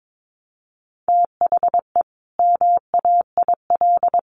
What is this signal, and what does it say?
Morse code, heard as a keyed tone, about 22 wpm, 700 Hz. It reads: THE MAIL
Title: Additional Easy Phrases 22wpm